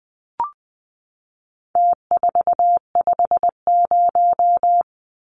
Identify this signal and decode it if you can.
Morse code, heard as a keyed tone, about 20 wpm, 700 Hz. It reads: T450
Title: Numbers - Contest Style - 4-Digit 20wpm